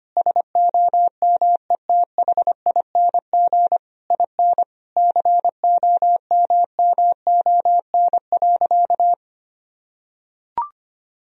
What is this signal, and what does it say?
Morse code, heard as a keyed tone, about 25 wpm, 700 Hz. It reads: SOMETHING IN COMMON.
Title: Sentences from Top 1000 Words - Review Part 3 - 25wpm